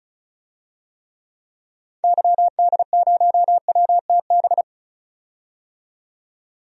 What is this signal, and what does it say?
Morse code, heard as a keyed tone, about 35 wpm, 700 Hz. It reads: YD0WTB
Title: DXCC-100 Call Signs - 6 Characters - 35wpm